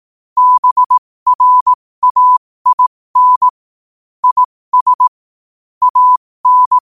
Morse code, heard as a keyed tone, about 18 wpm, spaced as for 15 wpm, 1000 Hz. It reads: BRAIN IS AN
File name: 15wpm